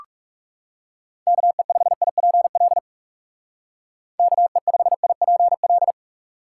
Morse code, heard as a keyed tone, about 36 wpm, 700 Hz. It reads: KE5IPL KE5IPL
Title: Speed Racing - USA 2x3 Call Signs 30wpm